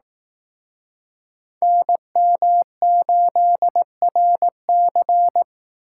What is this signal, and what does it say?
Morse code, heard as a keyed tone, about 18 wpm, 700 Hz. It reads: NM8RC
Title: Speed Racing - USA 2x2 Call Signs 15wpm